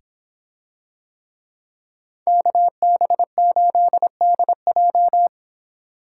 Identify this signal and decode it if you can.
Morse code, heard as a keyed tone, about 24 wpm, 700 Hz. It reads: KB8DJ